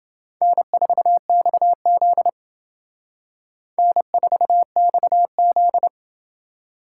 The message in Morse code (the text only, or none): N4XZ N4XZ